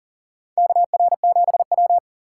Morse code, heard as a keyed tone, about 40 wpm, 700 Hz. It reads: KR7W
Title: K1USN SST QSO Practice - 40wpm